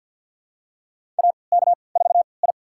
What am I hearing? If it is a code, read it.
Morse code, heard as a keyed tone, about 52 wpm, spaced as for 29 wpm, 700 Hz. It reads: AK4I
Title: USA Call Signs - 3x Character Spacing - 50wpm